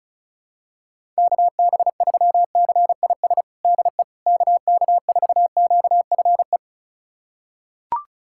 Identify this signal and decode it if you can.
Morse code, heard as a keyed tone, about 34 wpm, 700 Hz. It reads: KB3CIS DE KK4QFE
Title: USA Callsign DE Callsign - 6 Characters - 35wpm